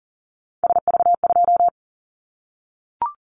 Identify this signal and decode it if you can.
Morse code, heard as a keyed tone, about 40 wpm, 700 Hz. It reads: SV2
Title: Callsign Prefixes for Top 50 DXCC - 40wpm